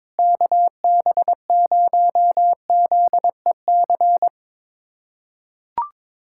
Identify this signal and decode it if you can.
Morse code, heard as a keyed tone, about 22 wpm, 700 Hz. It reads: KB0ZEC
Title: USA Callsign DE Callsign - 6 Characters - 22wpm